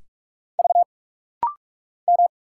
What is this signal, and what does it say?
Morse code, heard as a keyed tone, about 44 wpm, 700 Hz. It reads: V M
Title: Single Letters - Rapid-Fire 45wpm